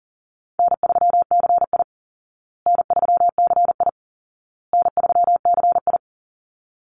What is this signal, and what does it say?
Morse code, heard as a keyed tone, about 40 wpm, 700 Hz. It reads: N3CI N3CI N3CI
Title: Call Signs Repeated 2x - 40wpm